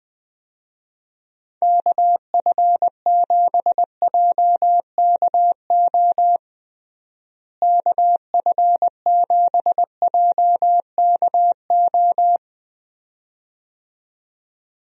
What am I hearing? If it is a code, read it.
Morse code, heard as a keyed tone, about 20 wpm, 700 Hz. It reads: KF7JKO KF7JKO
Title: USA Call Signs - 6 Characters - Repeated 2x - 20wpm